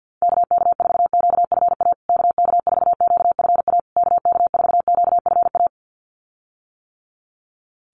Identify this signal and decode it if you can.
Morse code, heard as a keyed tone, about 50 wpm, 700 Hz. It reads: KK4QFA KK4QFA KK4QFA
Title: USA Call Signs Repeated 3x 50wpm